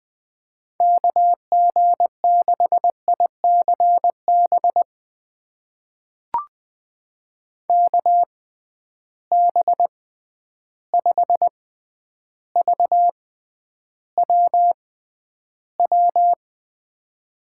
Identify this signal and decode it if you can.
Morse code, heard as a keyed tone, about 20 wpm, spaced as for 10 wpm, 700 Hz. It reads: KG6ICB KB5VWW
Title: USA Call Signs - 6x Character Spacing - 20wpm